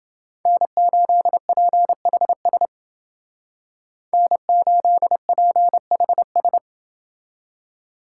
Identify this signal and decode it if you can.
Morse code, heard as a keyed tone, about 29 wpm, 700 Hz. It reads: N8PHS N8PHS